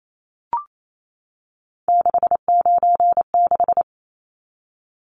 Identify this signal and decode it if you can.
Morse code, heard as a keyed tone, about 28 wpm, 700 Hz. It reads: 696
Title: Numbers - 3-Digit 28wpm